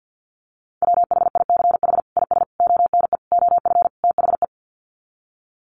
Morse code, heard as a keyed tone, about 50 wpm, 700 Hz. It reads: WHICH IS ONE OF THE
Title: Sets of 5 Words Encore 50wpm